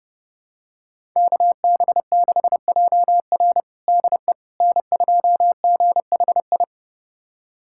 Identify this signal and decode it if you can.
Morse code, heard as a keyed tone, about 32 wpm, 700 Hz. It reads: KB6JR DE N2GHI
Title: USA Callsign DE Callsign 30wpm